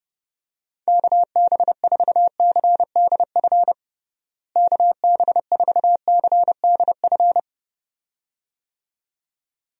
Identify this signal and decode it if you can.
Morse code, heard as a keyed tone, about 30 wpm, 700 Hz. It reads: KB4CDF KB4CDF
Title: USA Call Signs - 6 Characters - Repeated 2x - 30wpm